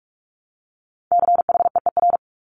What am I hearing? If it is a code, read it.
Morse code, heard as a keyed tone, about 46 wpm, 700 Hz. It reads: CHEER
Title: Multi-letter Phonemes Repeated 3x 45wpm